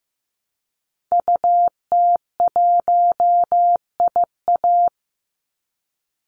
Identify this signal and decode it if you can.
Morse code, heard as a keyed tone, about 15 wpm, 700 Hz. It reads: UT1IA
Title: Call Signs Repeated 3x - 15wpm